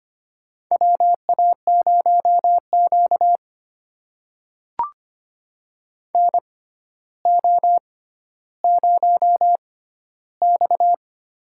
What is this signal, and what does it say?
Morse code, heard as a keyed tone, about 25 wpm, spaced as for 13 wpm, 700 Hz. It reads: WA0Q NO0X
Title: USA Call Signs - 4 Characters - 6x Character Spacing - 25wpm